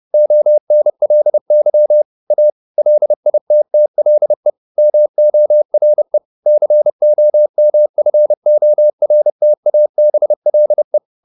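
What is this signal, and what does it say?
Morse code, heard as a keyed tone, about 30 wpm, 600 Hz. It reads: ONLY A LITTLE MORE COMFORTABLE
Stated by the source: Morse code at 30 wpm, the novel Never by Ken Follett in English, from CW med Gnister.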